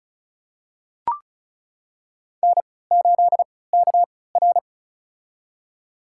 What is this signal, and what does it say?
Morse code, heard as a keyed tone, about 34 wpm, spaced as for 20 wpm, 700 Hz. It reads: N8KR
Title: USA Call Signs - 4 Characters - 3x Character Spacing - 35wpm